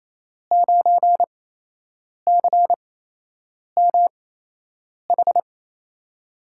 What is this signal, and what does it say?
Morse code, heard as a keyed tone, about 28 wpm, spaced as for 8 wpm, 700 Hz. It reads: 9CMH
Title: USA Call Signs - 5 Characters - 8x Character Spacing - 28wpm